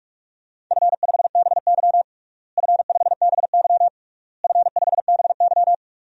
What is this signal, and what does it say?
Morse code, heard as a keyed tone, about 46 wpm, 700 Hz. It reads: F5BY F5BY F5BY
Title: DXCC-100 Call Signs - 4 Characters - Repeated 3x - 45wpm